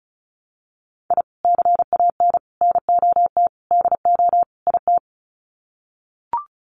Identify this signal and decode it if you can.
Morse code, heard as a keyed tone, about 34 wpm, 700 Hz. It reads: I CAN NOT DO IT
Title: Sets of 5 Words Encore 35wpm